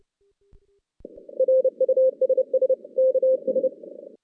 Morse code, heard as a keyed tone, about 30 wpm, 520 Hz. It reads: RUSS KS
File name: 2016 JAN CW NAQP / 2016 Jan naqp k5ot - 0-4